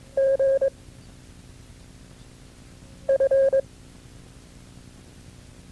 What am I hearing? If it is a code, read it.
Morse code, heard as a keyed tone, about 21 wpm, 540 Hz. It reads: G F
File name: FreeFiles / CW-practice / KB6MT-tape2-side1